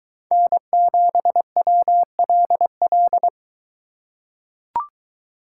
Morse code, heard as a keyed tone, about 21 wpm, 700 Hz. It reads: N7WLL